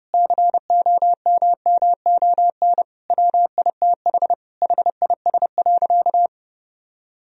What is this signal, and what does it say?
Morse code, heard as a keyed tone, about 30 wpm, 700 Hz. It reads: COMMON WITH HIS.
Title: Sentences from Top 1000 Words - Review Part 3 - 30wpm